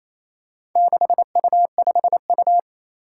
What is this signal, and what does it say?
Morse code, heard as a keyed tone, about 27 wpm, 700 Hz. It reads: 6U5U
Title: Four Challenging Letter-Number Combinations 28wpm